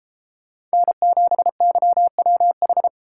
Morse code, heard as a keyed tone, about 33 wpm, 700 Hz. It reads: N7YWH